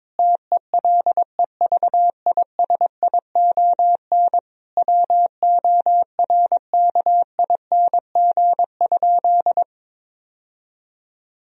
Morse code, heard as a keyed tone, about 22 wpm, 700 Hz. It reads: TELEVISION WORKING?
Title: Sentences from Top 1000 Words - Review Part 4 - 22wpm